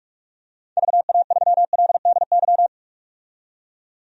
Morse code, heard as a keyed tone, about 43 wpm, 700 Hz. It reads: VA3LDY
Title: DXCC-100 Call Signs - 6 Characters - 45wpm